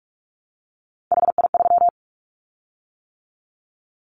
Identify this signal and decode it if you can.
Morse code, heard as a keyed tone, about 46 wpm, 700 Hz. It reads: HI3
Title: Callsign Prefixes for Top 100 DXCC - 45wpm